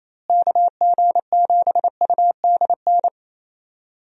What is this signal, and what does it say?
Morse code, heard as a keyed tone, about 28 wpm, 700 Hz. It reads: KG7UDN
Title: USA Call Signs - 6 Characters - Repeated 3x - 28wpm